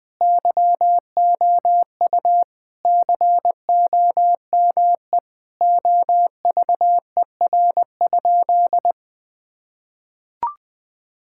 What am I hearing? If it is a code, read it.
Morse code, heard as a keyed tone, about 20 wpm, 700 Hz. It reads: YOU COME OVER?
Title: Sentences with 4-Word Phrases - Spoken Beginning - 20wpm